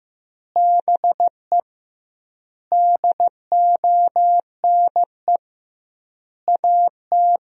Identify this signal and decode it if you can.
Morse code, heard as a keyed tone, about 15 wpm, 700 Hz. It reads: BE DONE AT
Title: 2x Word-Spacing - Sentences from Top 500 Words - 15wpm